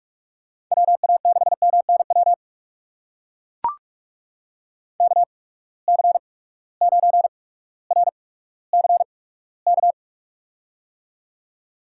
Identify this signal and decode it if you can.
Morse code, heard as a keyed tone, about 42 wpm, spaced as for 19 wpm, 700 Hz. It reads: WA6MNW KC9RCK
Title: USA Call Signs - 6 Characters - 8x Character Spacing - 45wpm